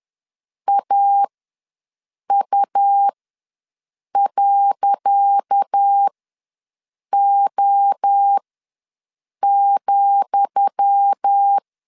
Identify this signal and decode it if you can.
Morse code, heard as a keyed tone, about 10 wpm, spaced as for 6 wpm, 800 Hz. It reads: AU.O,
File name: LEZIONI / 10 SIGNS / 10D COMA ws 10wpm 4314